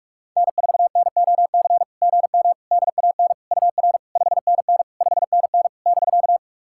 Morse code, heard as a keyed tone, about 41 wpm, 690 Hz. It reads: N4NOC GM DAN UR 5NN 5NN <BK>
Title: SOTA QSO Practice 45wpm